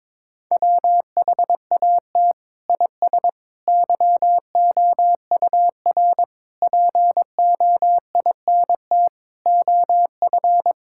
Morse code, heard as a keyed tone, about 22 wpm, 700 Hz. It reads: WHAT IS YOUR POINT OF
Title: Sentences from Top 1000 Words - Review Part 6 - 22wpm